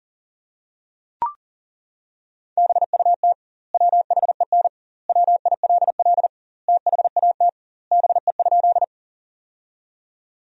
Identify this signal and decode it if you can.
Morse code, heard as a keyed tone, about 40 wpm, 700 Hz. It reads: BUT WHEN WILL THAT BE?
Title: 2x Word-Spacing - Sentences from Top 100 Words - 40wpm